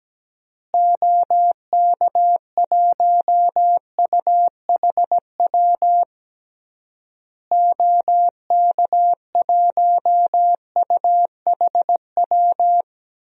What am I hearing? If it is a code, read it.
Morse code, heard as a keyed tone, about 17 wpm, 700 Hz. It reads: OK1UHW OK1UHW
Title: DXCC-100 Call Signs - 6 Characters - Repeated 3x - 17wpm